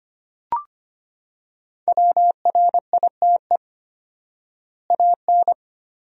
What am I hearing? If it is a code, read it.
Morse code, heard as a keyed tone, about 25 wpm, 700 Hz. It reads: WRITE AN
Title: 4x Word-Spacing - Sentences from Top 300 Words - 25wpm